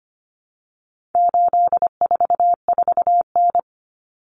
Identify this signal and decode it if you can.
Morse code, heard as a keyed tone, about 25 wpm, 700 Hz. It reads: 844N
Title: Common Cut Numbers - 4-Digit 25wpm